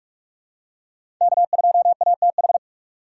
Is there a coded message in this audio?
Yes: Morse code from 1.2 to 2.6 s, heard as a keyed tone, about 41 wpm, 700 Hz. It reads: K2ATH